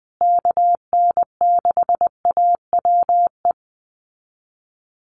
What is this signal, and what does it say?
Morse code, heard as a keyed tone, about 20 wpm, 700 Hz. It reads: KN6AWE
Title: USA Call Signs Repeated 3x 20wpm